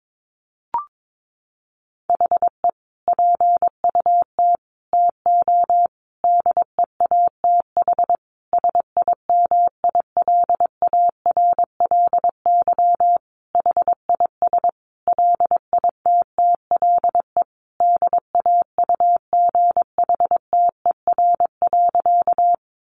Morse code, heard as a keyed tone, about 22 wpm, 700 Hz. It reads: HE PUT TO DEATH SIMILARLY HIS LITTLE DAUGHTER.
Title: Sentences from Top 1500 Words 22wpm